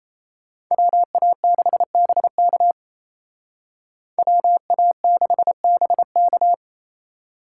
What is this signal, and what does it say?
Morse code, heard as a keyed tone, about 33 wpm, 700 Hz. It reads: WA6BK WA6BK
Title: Speed Racing - USA 2x2 Call Signs 28wpm